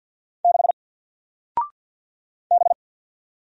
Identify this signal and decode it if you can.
Morse code, heard as a keyed tone, about 48 wpm, 700 Hz. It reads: 6 B
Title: Single Character (Letter:Number:Punctuation:Prosign) Alt 50wpm